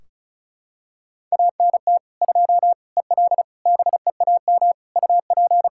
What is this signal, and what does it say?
Morse code, heard as a keyed tone, about 36 wpm, 700 Hz. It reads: ANT 2 EL BEAM UP
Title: Practice QSO - Exchange 2 - Info - 35wpm